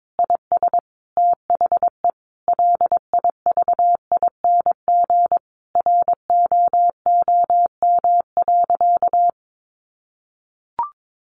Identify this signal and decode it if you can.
Morse code, heard as a keyed tone, about 22 wpm, 700 Hz. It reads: IS THE LIVING ROOM.
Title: Sentences from Top 1100 Words 22wpm